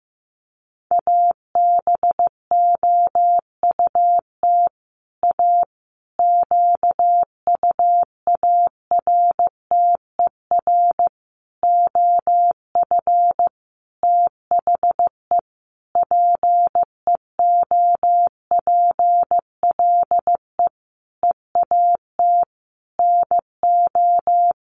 Morse code, heard as a keyed tone, about 14 wpm, 700 Hz. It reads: ABOUT A QUARTER OF THE PEOPLE EAT NO
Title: Sentences from Top 1300 Words 15wpm